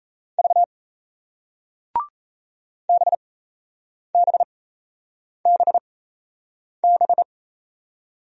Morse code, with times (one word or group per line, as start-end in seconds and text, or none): 0.4-0.6 V
2.9-7.2 BBBB